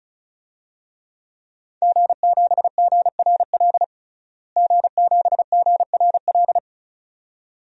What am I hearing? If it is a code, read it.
Morse code, heard as a keyed tone, about 34 wpm, 700 Hz. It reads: G7GRL G7GRL
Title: DXCC-100 Call Signs - 5 Characters - Repeated 2x - 35wpm